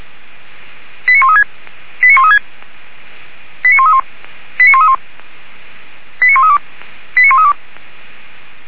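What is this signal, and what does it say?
Five-tone selective call (ZVEI), heard as quick runs of five tones, tones of about 70 ms. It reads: 89136 89136 89131 89131 89132 89132
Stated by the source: Five-tone ZVEI selective call received by SDR, archived by feuerwehr-orlamuende.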